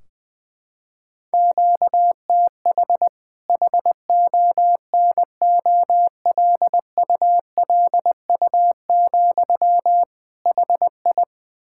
Morse code, heard as a keyed tone, about 20 wpm, 700 Hz. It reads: QTH HONOLULU, HI
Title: K1USN SST QSO Practice 20wpm